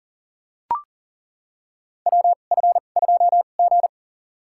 Morse code, heard as a keyed tone, about 40 wpm, spaced as for 29 wpm, 700 Hz. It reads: WF2G
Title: USA Call Signs - 2x Character Spacing - 40wpm